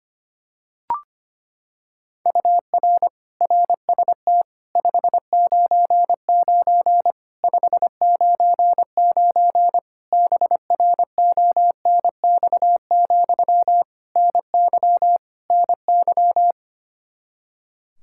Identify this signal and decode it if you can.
Morse code, heard as a keyed tone, about 25 wpm, 700 Hz. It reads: UR RST 599 599 BRONX, NY NY
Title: Practice QSO - Exchange 1 - Beginning - 25wpm